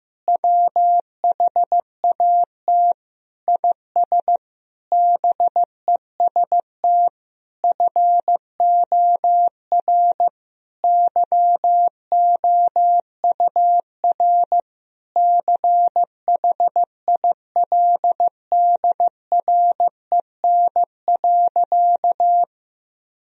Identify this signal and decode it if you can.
Morse code, heard as a keyed tone, about 15 wpm, 700 Hz. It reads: WHAT IS BEST FOR YOUR CHILDREN.
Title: Sentences from Top 1000 Words - Review Part 4 - 15wpm